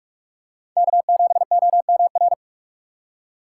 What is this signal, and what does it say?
Morse code, heard as a keyed tone, about 45 wpm, 700 Hz. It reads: K7OMR